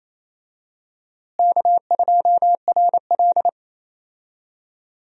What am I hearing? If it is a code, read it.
Morse code, heard as a keyed tone, about 28 wpm, 700 Hz. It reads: K2RL
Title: USA Call Signs - 4 Characters - Repeated 2x - 28wpm